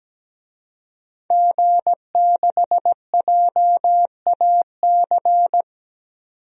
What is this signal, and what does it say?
Morse code, heard as a keyed tone, about 17 wpm, 700 Hz. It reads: G6JAC
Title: DXCC-100 Call Signs - 5 Characters - 17wpm